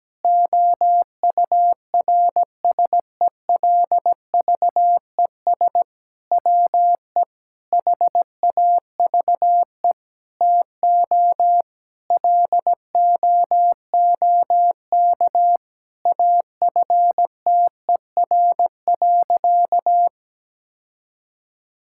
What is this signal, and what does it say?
Morse code, heard as a keyed tone, about 17 wpm, 700 Hz. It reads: OURSELVES WE HAVE TO LOOK AFTER.
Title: Sentences from Top 1600 Words 17wpm